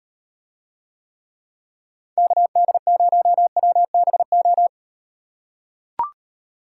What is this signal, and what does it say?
Morse code, heard as a keyed tone, about 38 wpm, 700 Hz. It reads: KD0WBO